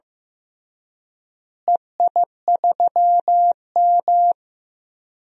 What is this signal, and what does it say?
Morse code, heard as a keyed tone, about 15 wpm, 700 Hz. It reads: EI3M
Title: DXCC-100 Call Signs - 4 Characters - Repeated 3x - 15wpm